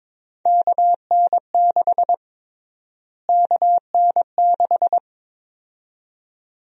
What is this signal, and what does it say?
Morse code, heard as a keyed tone, about 22 wpm, 700 Hz. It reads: KN6 KN6
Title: USA Call Sign Prefixes - Repeated 2x - 22wpm